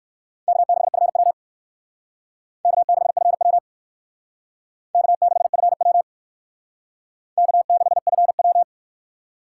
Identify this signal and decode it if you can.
Morse code, heard as a keyed tone, about 63 wpm, 700 Hz. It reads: K6FW K6FW K6FW K6FW